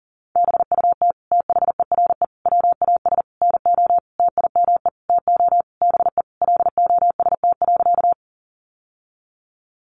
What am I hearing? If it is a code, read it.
Morse code, heard as a keyed tone, about 40 wpm, 700 Hz. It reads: BUT THERE WAS NO TIME TO BE LOST.
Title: Sentences from Top 1400 Words 40wpm